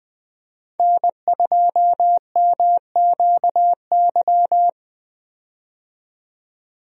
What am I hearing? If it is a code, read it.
Morse code, heard as a keyed tone, about 20 wpm, 700 Hz. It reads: N2MQY
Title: USA Call Signs - 5 Characters - 20wpm